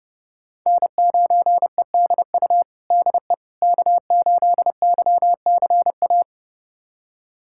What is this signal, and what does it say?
Morse code, heard as a keyed tone, about 30 wpm, 700 Hz. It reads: N9EDU DE K8YCA
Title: USA Callsign DE Callsign - 5 Characters - 30wpm